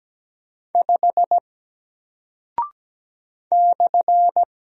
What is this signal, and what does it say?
Morse code, heard as a keyed tone, about 17 wpm, 700 Hz. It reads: 5 /
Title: Single Character (Letter:Number:Punctuation:Prosign) Alt 17wpm